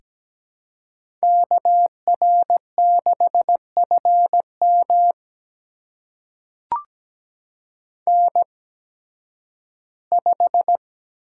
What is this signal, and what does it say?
Morse code, heard as a keyed tone, about 17 wpm, spaced as for 11 wpm, 700 Hz. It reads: KR6FM N5
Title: USA Call Signs - 5 Characters - 8x Character Spacing - 17wpm